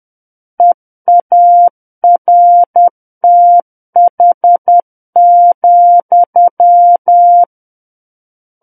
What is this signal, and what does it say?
Morse code, heard as a keyed tone, about 10 wpm, 700 Hz. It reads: EARTH,
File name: Book One / 01-Chapter 1 - 10 WPM